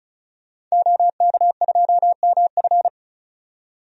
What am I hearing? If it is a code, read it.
Morse code, heard as a keyed tone, about 34 wpm, 700 Hz. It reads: OK2MF
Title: DXCC-100 Call Signs - 5 Characters - 35wpm